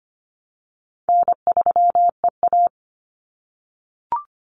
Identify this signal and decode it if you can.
Morse code, heard as a keyed tone, about 25 wpm, 700 Hz. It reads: N3EA
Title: Easy Call Signs - 25wpm